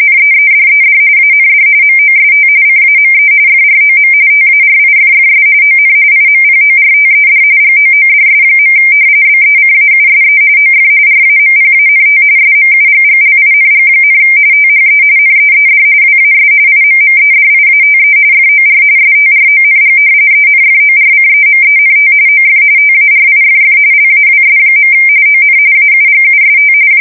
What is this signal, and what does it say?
RTTY, about 45.45 baud. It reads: NEEDED AND MOST CAME FROM TARIFFS ON IMPORTED GOODS. BECAUSE OF RAMPANT SMUGGLING, THE NEED WAS IMMEDIATE FOR STRONG ENFORCEMENT OF TARIFF LAWS. ON AUGUST
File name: fsk170 / usnavy1